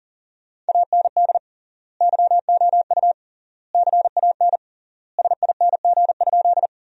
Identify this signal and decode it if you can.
Morse code, heard as a keyed tone, about 40 wpm, 700 Hz. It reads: AND YOU CAN SING?
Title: 3x Word-Spacing - Sentences from Top 400 Words - 40wpm